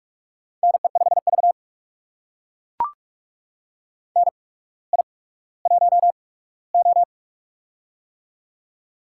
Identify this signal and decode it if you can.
Morse code, heard as a keyed tone, about 44 wpm, spaced as for 19 wpm, 690 Hz. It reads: NE5V NI1O
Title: USA Call Signs - 4 Characters - 8x Character Spacing - 45wpm